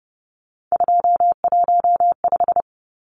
Morse code, heard as a keyed tone, about 30 wpm, 700 Hz. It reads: 215
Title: Numbers - 3-Digit 30wpm